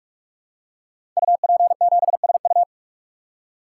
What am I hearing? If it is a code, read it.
Morse code, heard as a keyed tone, about 44 wpm, 700 Hz. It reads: UP7SU
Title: Code Groups - 5 Letter:Numbers - 45wpm